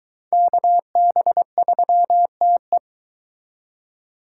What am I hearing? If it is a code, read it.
Morse code, heard as a keyed tone, about 22 wpm, 700 Hz. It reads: KB3TE